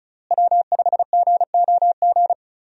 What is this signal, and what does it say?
Morse code, heard as a keyed tone, about 34 wpm, 700 Hz. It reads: W5GOG
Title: USA Call Signs - 5 Characters - Repeated 2x - 35wpm